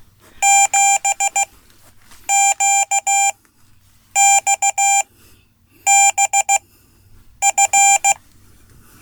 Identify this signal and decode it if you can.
Morse code, heard as a keyed tone, about 16 wpm, spaced as for 8 wpm, 790 Hz. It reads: 7QXBF
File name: Code / 5min-a